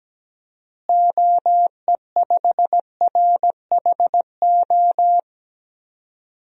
Morse code, heard as a keyed tone, about 17 wpm, 700 Hz. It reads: OE5RHO
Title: DXCC-100 Call Signs - 6 Characters - Repeated 3x - 17wpm